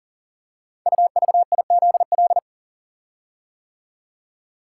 Morse code, heard as a keyed tone, about 38 wpm, 700 Hz. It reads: UVIZL
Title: Code Groups - 5 Letters - 40wpm